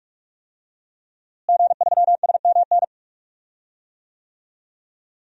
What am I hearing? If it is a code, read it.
Morse code, heard as a keyed tone, about 44 wpm, 700 Hz. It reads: G3SMN
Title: DXCC-100 Call Signs - 5 Characters - Repeated 3x - 45wpm